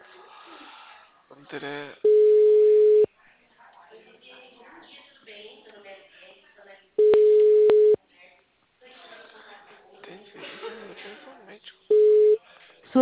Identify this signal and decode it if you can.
Ringback tone (Europe, 425 Hz), with speech.